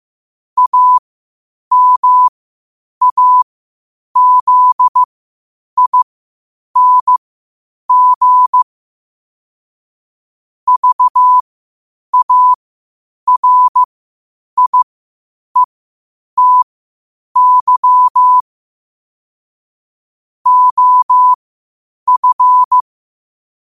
Morse code, heard as a keyed tone, about 15 wpm, spaced as for 8 wpm, 1000 Hz. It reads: AMAZING VARIETY OF
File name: qotdfiles / 8wpm